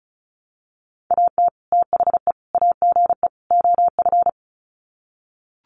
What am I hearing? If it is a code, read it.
Morse code, heard as a keyed tone, about 34 wpm, 700 Hz. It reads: AT THE AGE OF
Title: Sets of 4 Words 35wpm